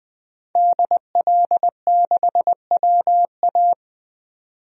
Morse code, heard as a keyed tone, about 20 wpm, 700 Hz. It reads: DL6WA
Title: DXCC-100 Call Signs - 5 Characters - 20wpm